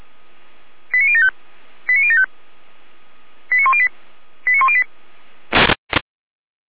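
Five-tone selective call (ZVEI), heard as quick runs of five tones, tones of about 70 ms. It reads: 89075 89075 89108 89108
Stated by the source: ZVEI five-tone selective call received by SDR.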